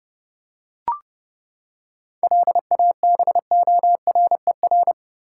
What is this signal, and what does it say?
Morse code, heard as a keyed tone, about 31 wpm, 700 Hz. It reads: LABORER
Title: K1USN SST QSO Practice 30wpm